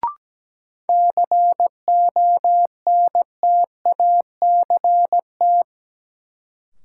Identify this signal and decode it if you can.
Morse code, heard as a keyed tone, about 17 wpm, 700 Hz. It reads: CONTACT
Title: Sentences from Top 1000 Words - Review Part 7 - New Word Drill - 17wpm